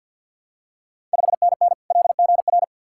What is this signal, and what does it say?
Morse code, heard as a keyed tone, about 53 wpm, 690 Hz. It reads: 5NN LGR
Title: Ontario QSO Party Exchange 50wpm